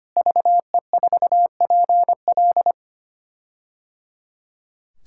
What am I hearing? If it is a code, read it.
Morse code, heard as a keyed tone, about 25 wpm, 700 Hz. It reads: VE4PL
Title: K1USN SST QSO Practice - 25wpm